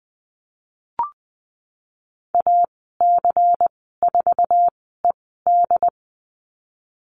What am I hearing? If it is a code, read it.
Morse code, heard as a keyed tone, about 20 wpm, spaced as for 15 wpm, 700 Hz. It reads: AC4ED
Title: 147 Lesson 24g - Letter y - Call Signs 20wpm